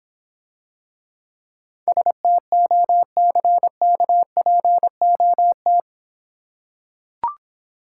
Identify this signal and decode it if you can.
Morse code, heard as a keyed tone, about 26 wpm, 700 Hz. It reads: STOCKPOT